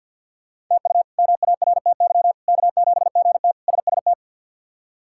Morse code, heard as a keyed tone, about 50 wpm, 700 Hz. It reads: TU MARTY K7GT SST
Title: K1USN SST QSO Practice - 50wpm